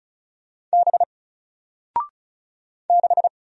Morse code, heard as a keyed tone, about 35 wpm, 700 Hz. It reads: B 6
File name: Single-Letter-Number-35wpm